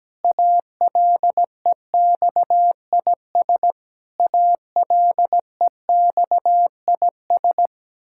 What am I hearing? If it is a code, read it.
Morse code, heard as a keyed tone, about 17 wpm, 700 Hz. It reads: ALEXIS ALEXIS
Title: Practice QSO - Exchange 1 - Beginning - 17wpm